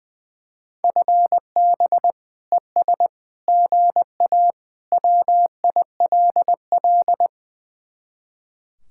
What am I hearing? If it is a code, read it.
Morse code, heard as a keyed tone, about 20 wpm, 700 Hz. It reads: FB ES GA WILL
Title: Speed-Racing Common QSO Phrases - 20wpm